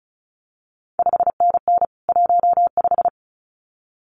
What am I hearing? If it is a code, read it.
Morse code, heard as a keyed tone, about 36 wpm, 700 Hz. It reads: 5NN 15
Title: CQ WW DX Contest Exchange - 35wpm